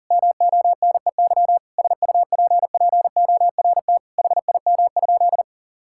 Morse code, heard as a keyed tone, about 40 wpm, 700 Hz. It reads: MONEY SUPPORT HIM?
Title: Sentences from Top 1000 Words - Review Part 5 - 40wpm